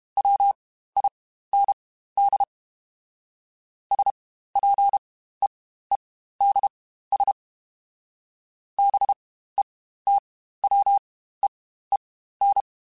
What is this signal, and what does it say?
Morse code, heard as a keyed tone, about 32 wpm, spaced as for 15 wpm, 800 Hz. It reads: WIND SPEEDS BETWEEN
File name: Spaced Characters / Spaced Characters-mp3s / Weather 4 24 sp @30